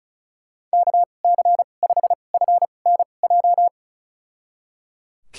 Morse code, heard as a keyed tone, about 34 wpm, spaced as for 25 wpm, 700 Hz. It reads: KC5FNJ
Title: USA Call Signs - 2x Character Spacing - 35wpm